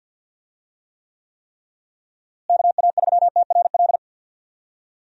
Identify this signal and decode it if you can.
Morse code, heard as a keyed tone, about 51 wpm, 700 Hz. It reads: KA3TRL
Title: USA Call Signs - 6 Characters - Repeated 3x - 50wpm